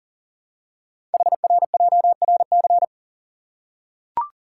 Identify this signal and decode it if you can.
Morse code, heard as a keyed tone, about 42 wpm, 700 Hz. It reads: HRJRC
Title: Code Groups - 5 Letters - 40wpm